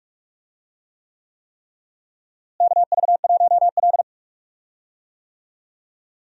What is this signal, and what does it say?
Morse code, heard as a keyed tone, about 45 wpm, 700 Hz. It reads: KV1L
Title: USA Call Signs - 4 Characters - Repeated 3x - 45wpm